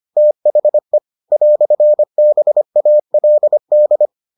Morse code, heard as a keyed tone, about 25 wpm, 600 Hz. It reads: THE "BALD